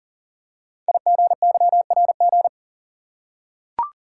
Morse code, heard as a keyed tone, about 40 wpm, 690 Hz. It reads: IGYRG
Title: Code Groups - 5 Letters - 40wpm